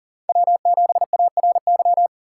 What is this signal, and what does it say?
Morse code, heard as a keyed tone, about 40 wpm, 690 Hz. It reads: W7ARY
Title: USA Call Signs - 6x Character Spacing - 40wpm